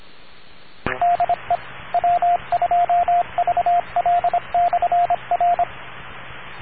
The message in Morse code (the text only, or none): DE W2VL/R